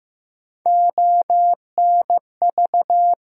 Morse code, heard as a keyed tone, about 15 wpm, 700 Hz. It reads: ONV
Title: USA Call Sign Postfixes - 3 Letters 15wpm